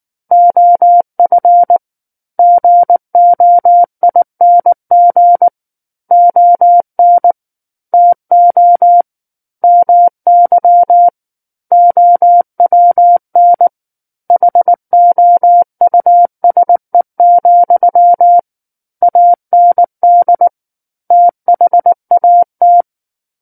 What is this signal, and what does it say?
Morse code, heard as a keyed tone, about 19 wpm, 700 Hz. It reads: OF GOING ON TO MY OWN HOUSE, AND THAT
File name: Chapter 10 - 19 WPM